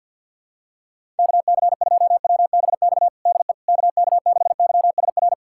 Morse code, heard as a keyed tone, about 52 wpm, 700 Hz. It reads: KC2WBX DE KK6YSR
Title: USA Callsign DE Callsign - 6 Characters - 50wpm